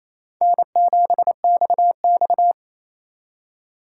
Morse code, heard as a keyed tone, about 34 wpm, 700 Hz. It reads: N7XX
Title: Speed Racing - USA 1x2 Call Signs 28wpm